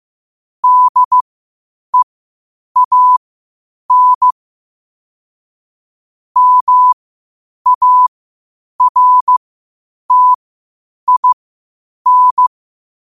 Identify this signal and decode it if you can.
Morse code, heard as a keyed tone, about 15 wpm, spaced as for 8 wpm, 1000 Hz. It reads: DEAN MARTIN
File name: qotdfiles / 8wpm